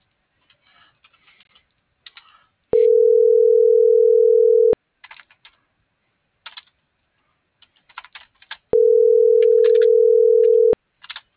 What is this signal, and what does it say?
Ringback tone (North America), 440 + 480 Hz.